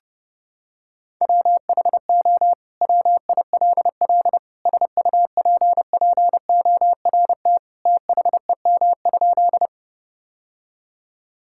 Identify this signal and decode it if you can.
Morse code, heard as a keyed tone, about 30 wpm, 700 Hz. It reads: WHO WILL SUPPORT THEM?
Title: Sentences from Top 1000 Words - Review Part 5 - 30wpm